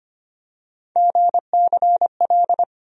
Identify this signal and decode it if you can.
Morse code, heard as a keyed tone, about 25 wpm, 700 Hz. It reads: GCL
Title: USA Call Sign Postfixes - 3 Letters 25wpm